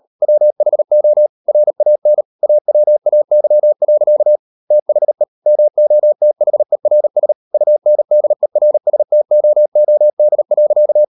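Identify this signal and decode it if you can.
Morse code, heard as a keyed tone, about 38 wpm, 600 Hz. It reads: WHO RAN AWAY. THE MOTHERS UNDERSTOOD.
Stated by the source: Morse code practice at 38 WPM, sending Ken Follett's novel Never in English, from CW med Gnister.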